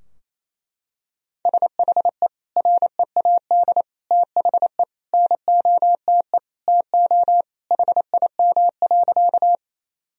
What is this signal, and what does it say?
Morse code, heard as a keyed tone, about 28 wpm, 700 Hz. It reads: SHE READ THE NOTE TO HIM.
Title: Sentences from Top 1000 Words - Review Part 2 - 28wpm